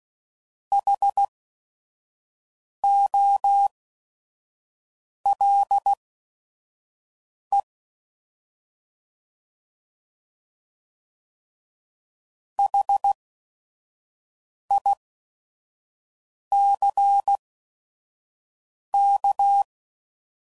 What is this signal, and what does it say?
Morse code, heard as a keyed tone, about 16 wpm, spaced as for 5 wpm, 800 Hz. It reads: HOLE HICK
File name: Prefixes / By First Letter / Hs